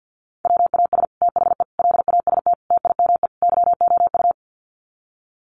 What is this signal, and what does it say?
Morse code, heard as a keyed tone, about 50 wpm, 700 Hz. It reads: WAS THE LAST TIME YOU
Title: Sets of 5 Words Encore2 50wpm